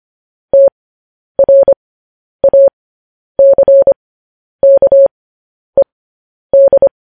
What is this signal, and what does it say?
Morse code, heard as a keyed tone, about 25 wpm, spaced as for 10 wpm, 550 Hz. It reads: TRACKED
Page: news-headlines-in-morse-code-at-10-wpm